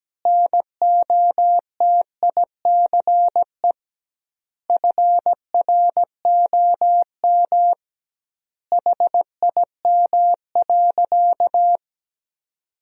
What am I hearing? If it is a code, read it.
Morse code, heard as a keyed tone, about 17 wpm, 700 Hz. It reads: NOTICE FROM HIM.
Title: 2x Word-Spacing - Sentences from Top 400 Words - 17wpm